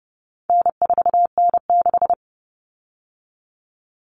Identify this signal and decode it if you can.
Morse code, heard as a keyed tone, about 30 wpm, 700 Hz. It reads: N4N6
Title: Common Cut Numbers - 4-Digit 30wpm